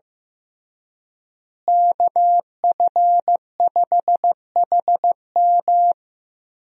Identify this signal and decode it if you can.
Morse code, heard as a keyed tone, about 15 wpm, 700 Hz. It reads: KF5HM
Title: USA Call Signs - 5 Characters - Repeated 3x - 15wpm